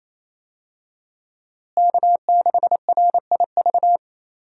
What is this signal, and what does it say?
Morse code, heard as a keyed tone, about 28 wpm, 700 Hz. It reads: K6RIV
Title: USA Call Signs - 5 Characters - Repeated 3x - 28wpm